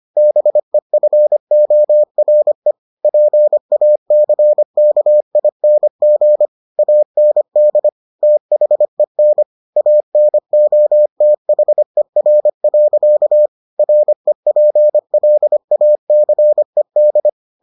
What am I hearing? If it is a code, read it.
Morse code, heard as a keyed tone, about 25 wpm, 600 Hz. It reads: BEFORE PACKING AND THEN ANOTHER. REPLACED